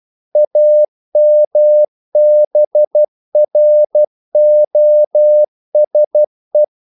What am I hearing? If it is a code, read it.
Morse code, heard as a keyed tone, about 12 wpm, 600 Hz. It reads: AMBROSE
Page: English quotations in CW (telegraphy - morse)